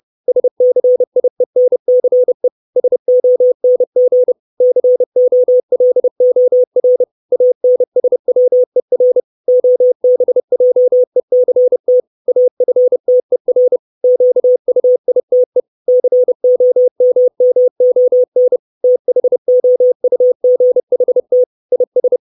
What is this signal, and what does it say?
Morse code, heard as a keyed tone, about 30 wpm, 500 Hz. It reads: SCIENCE SONG COLOR ANSWER OBJECT AFTER QUITE COMMON THOUGHT IS
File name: 632 Words 30WPM